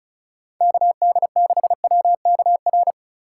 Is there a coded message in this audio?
Yes: Morse code from 0.6 to 2.9 s, heard as a keyed tone, about 44 wpm, 700 Hz. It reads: KD6WKR